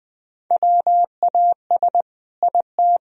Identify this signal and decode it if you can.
Morse code, heard as a keyed tone, about 20 wpm, 700 Hz. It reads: WAS IT
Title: Sentences from Top 1000 Words - Review Part 3 - 20wpm